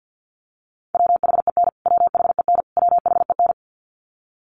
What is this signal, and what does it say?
Morse code, heard as a keyed tone, about 49 wpm, 700 Hz. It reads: WHEN WHEN WHEN
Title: Top 300 Words Repeated 50wpm